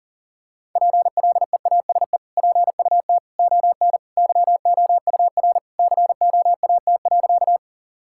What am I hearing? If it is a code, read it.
Morse code, heard as a keyed tone, about 40 wpm, 700 Hz. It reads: PLEASE PUT ON YOUR COAT.
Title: Sentences from Top 1600 Words 40wpm